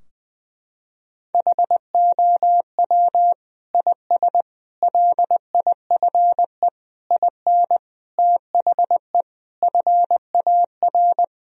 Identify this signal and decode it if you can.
Morse code, heard as a keyed tone, about 20 wpm, 700 Hz. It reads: HOW IS LIFE IN THE FAR
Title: Sentences from Top 1000 Words - Review Part 2 - 20wpm